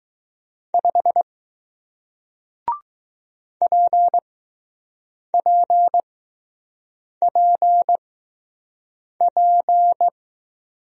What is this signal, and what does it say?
Morse code, heard as a keyed tone, about 20 wpm, spaced as for 6 wpm, 700 Hz. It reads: 5 PPPP